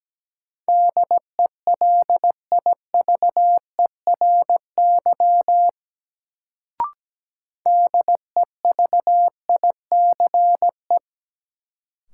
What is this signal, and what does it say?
Morse code, heard as a keyed tone, about 17 wpm, 700 Hz. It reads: DELIVERY DEVICE
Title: Sentences from Top 1500 Words - New Word Drill Alt - 17wpm